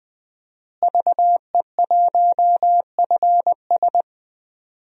Morse code, heard as a keyed tone, about 20 wpm, 700 Hz. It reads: VE1FS
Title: DXCC-100 Call Signs - 5 Characters - Repeated 2x - 20wpm